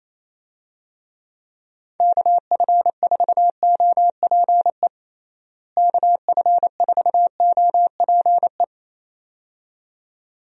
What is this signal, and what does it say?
Morse code, heard as a keyed tone, about 28 wpm, 700 Hz. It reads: KF4OPE KF4OPE
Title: USA Call Signs - 6 Characters - Repeated 2x - 28wpm